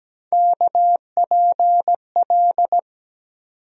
Morse code, heard as a keyed tone, about 17 wpm, 700 Hz. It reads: KPL
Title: USA Call Sign Postfixes - 3 Letters 17wpm